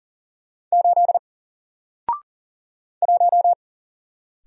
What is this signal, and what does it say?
Morse code, heard as a keyed tone, about 40 wpm, 700 Hz. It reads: 8 1
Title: Single Character (Letter:Number:Punctuation:Prosign) Alt 40wpm